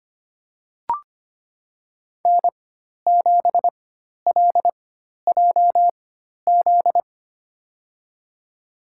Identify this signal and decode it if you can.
Morse code, heard as a keyed tone, about 25 wpm, spaced as for 12 wpm, 700 Hz. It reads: N7LJZ
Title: USA Call Signs - 5 Characters - 4x Character Spacing - 25wpm